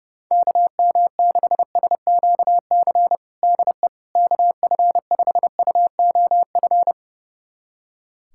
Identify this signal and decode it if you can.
Morse code, heard as a keyed tone, about 30 wpm, 700 Hz. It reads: KM6SQC DE KF5UOF
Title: USA Callsign DE Callsign - 6 Characters - 30wpm